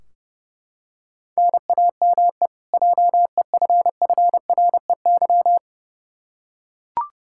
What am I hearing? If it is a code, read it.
Morse code, heard as a keyed tone, about 30 wpm, 700 Hz. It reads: NAME JEFFREY
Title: Practice QSO - Exchange 1 - Beginning - 30wpm